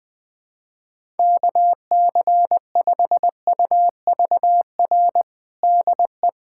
Morse code, heard as a keyed tone, about 20 wpm, 700 Hz. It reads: KC5UVR DE
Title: USA Callsign DE Callsign 20wpm